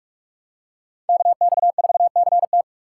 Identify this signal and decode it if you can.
Morse code, heard as a keyed tone, about 46 wpm, 700 Hz. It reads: KX4CT
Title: USA Call Signs - 6x Character Spacing - 45wpm